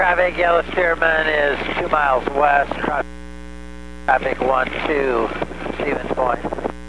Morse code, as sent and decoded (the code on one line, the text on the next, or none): none